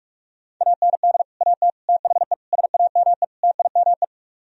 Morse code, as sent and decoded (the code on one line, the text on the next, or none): .- -. -.. / .- - / - .... / ... .- -- . / - .. -- .
AND AT THE SAME TIME